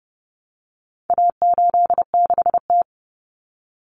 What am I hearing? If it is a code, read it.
Morse code, heard as a keyed tone, about 30 wpm, 700 Hz. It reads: A86T
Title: Common Cut Numbers - 4-Digit 30wpm